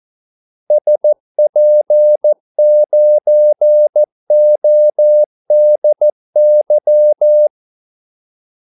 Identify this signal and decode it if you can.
Morse code, heard as a keyed tone, about 14 wpm, 600 Hz. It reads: SP9ODY
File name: Callsigns-14wpm